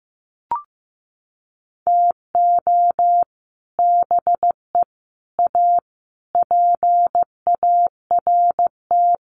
Morse code, heard as a keyed tone, about 15 wpm, 700 Hz. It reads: TO BE A PART
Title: Sets of 5 Words 15wpm